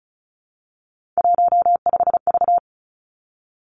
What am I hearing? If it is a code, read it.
Morse code, heard as a keyed tone, about 36 wpm, 700 Hz. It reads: 15V
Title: Three Challenging Letter-Number Combinations 35wpm